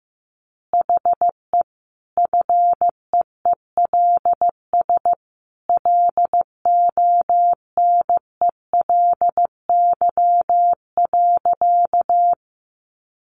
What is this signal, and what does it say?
Morse code, heard as a keyed tone, about 15 wpm, 700 Hz. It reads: HE FEELS LONELY.
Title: Easy Phrases 15wpm